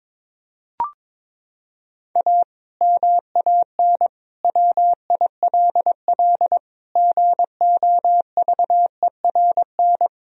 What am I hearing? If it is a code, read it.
Morse code, heard as a keyed tone, about 22 wpm, 700 Hz. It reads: A MAN WILL GOVERN
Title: Sentences from Top 1000 Words - Review Part 2 - 22wpm